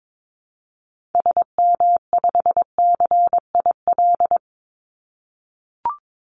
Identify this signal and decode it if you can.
Morse code, heard as a keyed tone, about 22 wpm, 700 Hz. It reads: SM5CIL
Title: Call Signs - 22wpm